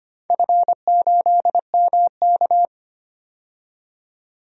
Morse code, heard as a keyed tone, about 25 wpm, 700 Hz. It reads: F8MK
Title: DXCC-100 Call Signs - 4 Characters - 25wpm